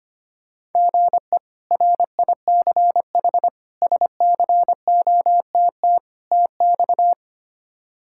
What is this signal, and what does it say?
Morse code, heard as a keyed tone, about 25 wpm, 700 Hz. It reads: GE RICH SCOTT TX
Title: K1USN SST QSO Practice - 25wpm